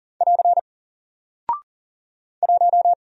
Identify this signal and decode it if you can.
Morse code, heard as a keyed tone, about 40 wpm, 700 Hz. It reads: <AR> 1
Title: Single Character (Letter:Number:Punctuation:Prosign) Alt 40wpm